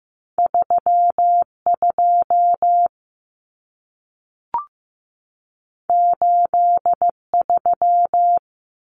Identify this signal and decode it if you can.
Morse code, heard as a keyed tone, about 15 wpm, 700 Hz. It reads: 32 83
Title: Common Cut Numbers - 2-Digit 15wpm